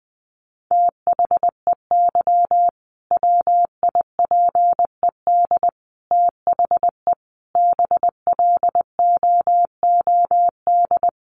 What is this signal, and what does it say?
Morse code, heard as a keyed tone, about 20 wpm, 700 Hz. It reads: THEY WIPED THE BLOOD
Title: A Sound Of Thunder 20wpm